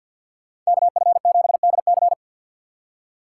Morse code, heard as a keyed tone, about 61 wpm, 700 Hz. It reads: KF7DC
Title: Speed Racing - USA 2x2 Call Signs 50wpm